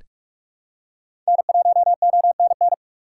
Morse code, heard as a keyed tone, about 46 wpm, 700 Hz. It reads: N1ONN
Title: USA Call Signs - 5 Characters - 4x Character Spacing - 45wpm